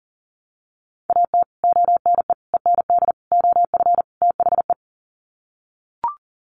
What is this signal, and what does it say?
Morse code, heard as a keyed tone, about 37 wpm, 700 Hz. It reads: AT ONE END OF THE
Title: Sets of 5 Words Encore 40wpm